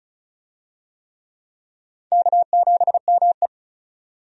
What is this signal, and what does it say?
Morse code, heard as a keyed tone, about 36 wpm, 700 Hz. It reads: K7ME
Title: USA Call Signs - 4 Characters - 35wpm